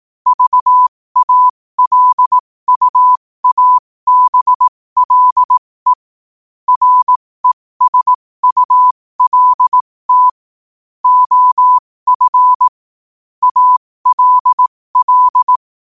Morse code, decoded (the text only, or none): VALUABLE RESULT OF ALL